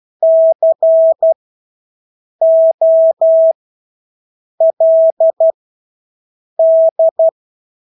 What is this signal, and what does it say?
Morse code, heard as a keyed tone, about 12 wpm, spaced as for 6 wpm, 650 Hz. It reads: COLD
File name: Engelske ord 12-6 wpm